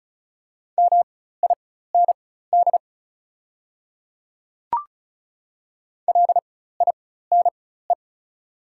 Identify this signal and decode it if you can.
Morse code, heard as a keyed tone, about 35 wpm, spaced as for 16 wpm, 690 Hz. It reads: MIND LINE
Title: 4x Character Spacing - Top 500 Words - 35wpm